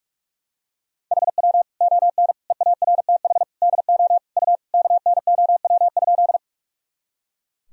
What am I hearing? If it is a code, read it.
Morse code, heard as a keyed tone, about 41 wpm, 700 Hz. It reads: HW ON EARTH DO U KNOW?
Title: Sentences from Top 200 Words and Abbreviations 45wpm